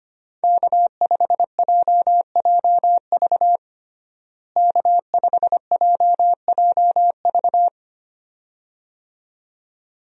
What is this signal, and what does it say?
Morse code, heard as a keyed tone, about 25 wpm, 700 Hz. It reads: K5JJV K5JJV
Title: USA Call Signs - 5 Characters - Repeated 2x - 25wpm